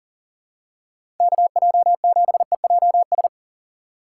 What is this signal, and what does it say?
Morse code, heard as a keyed tone, about 39 wpm, 690 Hz. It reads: KJ7EJS
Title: USA Call Signs - 8x Character Spacing - 40wpm